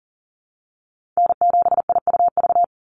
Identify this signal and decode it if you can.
Morse code, heard as a keyed tone, about 40 wpm, 690 Hz. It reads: N7IUV
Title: USA Call Signs 40wpm